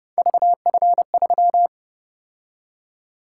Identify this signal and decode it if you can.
Morse code, heard as a keyed tone, about 31 wpm, 700 Hz. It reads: VF3